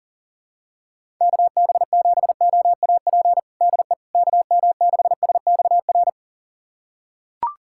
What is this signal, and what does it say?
Morse code, heard as a keyed tone, about 40 wpm, 700 Hz. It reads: KB7OAP DE KM6SXR
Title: USA Callsign DE Callsign - 6 Characters - 40wpm